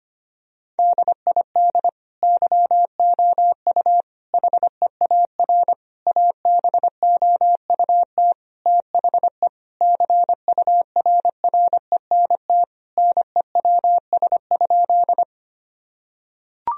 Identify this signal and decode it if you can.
Morse code, heard as a keyed tone, about 25 wpm, 700 Hz. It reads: DID YOU HEAR ABOUT THE CURRENT NEWS?
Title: Sentences from Top 1000 Words - Review Part 4 - 25wpm